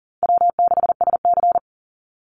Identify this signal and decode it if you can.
Morse code, heard as a keyed tone, about 40 wpm, 700 Hz. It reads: W6SC
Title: Easy Call Signs - 40wpm